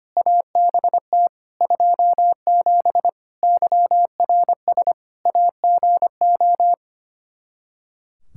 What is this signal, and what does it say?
Morse code, heard as a keyed tone, about 25 wpm, 700 Hz. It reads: ABT 27 YRS AGO
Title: Practice QSO - Exchange 3 - More Info - 25wpm